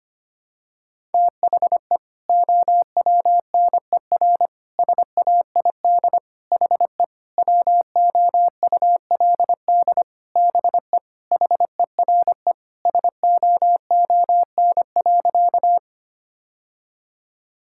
Morse code, heard as a keyed tone, about 25 wpm, 700 Hz. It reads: THE OWNER SAID HE WOULD BE HERE SOON.
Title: Sentences from Top 1600 Words 25wpm